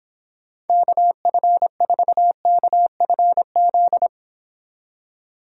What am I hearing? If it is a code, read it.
Morse code, heard as a keyed tone, about 26 wpm, 700 Hz. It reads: KF4KFZ